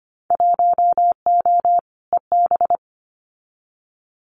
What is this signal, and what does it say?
Morse code, heard as a keyed tone, about 25 wpm, 700 Hz. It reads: 1O EB
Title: Winter Field Day Exchanges 25wpm